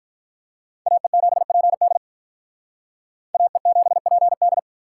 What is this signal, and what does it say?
Morse code, heard as a keyed tone, about 43 wpm, 700 Hz. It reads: AE7PD AE7PD
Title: Speed Racing - USA 2x2 Call Signs 35wpm